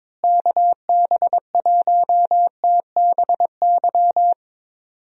Morse code, heard as a keyed tone, about 22 wpm, 700 Hz. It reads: KB1TBY
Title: USA Call Signs - 6 Characters - Repeated 3x - 22wpm